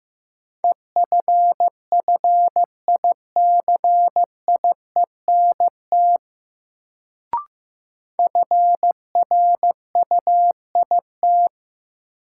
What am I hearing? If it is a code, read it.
Morse code, heard as a keyed tone, about 15 wpm, 700 Hz. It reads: EFFICIENT FRUIT
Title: Sentences from Top 1300 Words - New Word Drill Alt - 15wpm